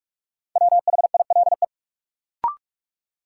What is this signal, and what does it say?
Morse code, heard as a keyed tone, about 45 wpm, 700 Hz. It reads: WHILE
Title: Sentences from Top 300 Words - New Word Drill - 45wpm